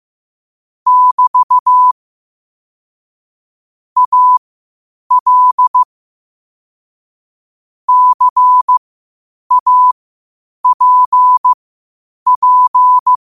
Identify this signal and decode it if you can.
Morse code, heard as a keyed tone, about 15 wpm, spaced as for 8 wpm, 1000 Hz. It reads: <BT> AL CAPP
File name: qotdfiles / 8wpm